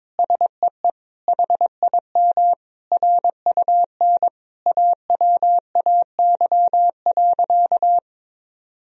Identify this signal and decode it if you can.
Morse code, heard as a keyed tone, about 22 wpm, 700 Hz. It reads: SEE HIM RUN AWAY.
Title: Sentences from Top 1000 Words - Review Part 1 - 22wpm